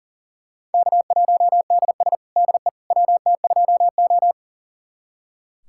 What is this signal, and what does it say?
Morse code, heard as a keyed tone, about 37 wpm, 700 Hz. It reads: K1DS DE WT2O
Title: USA Callsign DE Callsign - 4 Characters - 40wpm